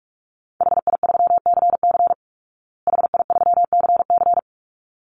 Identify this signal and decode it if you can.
Morse code, heard as a keyed tone, about 44 wpm, 700 Hz. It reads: HI3CC HI3CC
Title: Call Signs Repeated 3x - 45wpm